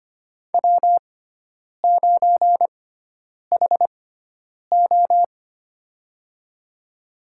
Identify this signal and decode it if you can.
Morse code, heard as a keyed tone, about 25 wpm, spaced as for 9 wpm, 700 Hz. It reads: W9HO
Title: USA Call Signs - 4 Characters - 6x Character Spacing - 25wpm